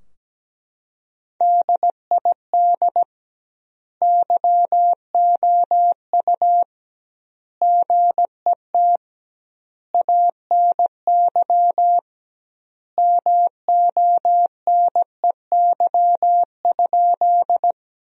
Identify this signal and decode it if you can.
Morse code, heard as a keyed tone, about 17 wpm, 700 Hz. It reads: DID YOU GET ANY MONEY?
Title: 2x Word-Spacing - Sentences from Top 400 Words - 17wpm